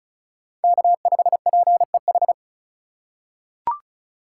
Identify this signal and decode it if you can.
Morse code, heard as a keyed tone, about 36 wpm, 700 Hz. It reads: K5PEH
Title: USA Call Signs - 5 Characters - 4x Character Spacing - 35wpm